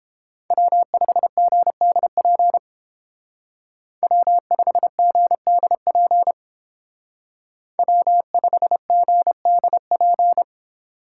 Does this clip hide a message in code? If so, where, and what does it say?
Yes: Morse code from 0.5 to 10.4 s, heard as a keyed tone, about 29 wpm, 700 Hz. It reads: W5GDP W5GDP W5GDP